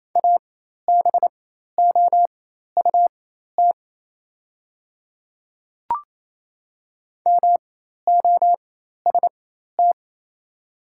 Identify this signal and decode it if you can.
Morse code, heard as a keyed tone, about 28 wpm, spaced as for 13 wpm, 700 Hz. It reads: ABOUT MOST
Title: 4x Character Spacing - Top 100 Words - 28wpm